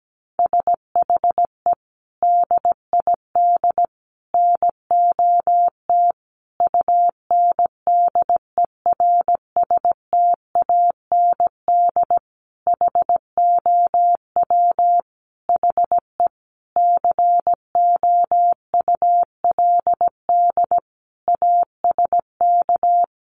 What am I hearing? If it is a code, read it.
Morse code, heard as a keyed tone, about 17 wpm, 700 Hz. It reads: SHE DID NOT UNDERSTAND HOW HE COULD ASK
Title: Sentences from Top 600 Words 17wpm